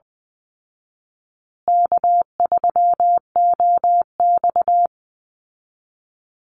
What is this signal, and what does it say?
Morse code, heard as a keyed tone, about 20 wpm, 700 Hz. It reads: K3OX
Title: Call Signs Repeated 2x - 20wpm